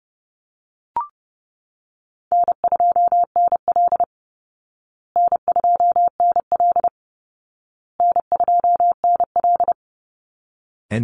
Morse code, heard as a keyed tone, about 30 wpm, 700 Hz. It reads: N2NL N2NL N2NL
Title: Call Signs Repeated 2x - 30wpm